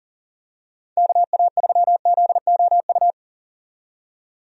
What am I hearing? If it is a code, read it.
Morse code, heard as a keyed tone, about 40 wpm, 700 Hz. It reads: KA3ZOU
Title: USA Call Signs - 6 Characters - 40wpm